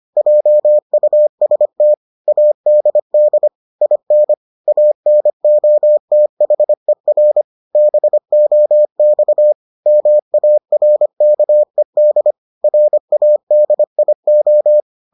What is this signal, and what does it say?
Morse code, heard as a keyed tone, about 25 wpm, 600 Hz. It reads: JUST ADD IN ANOTHER BOX MARKED RADIO